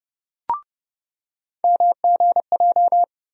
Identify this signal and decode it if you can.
Morse code, heard as a keyed tone, about 30 wpm, 700 Hz. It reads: MGJ
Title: USA Call Sign Postfixes - 3 Letters 30wpm